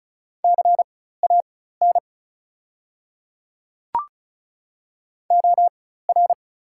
Morse code, heard as a keyed tone, about 34 wpm, spaced as for 16 wpm, 690 Hz. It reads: CAN OR
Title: 4x Character Spacing - Top 200 Words - 35wpm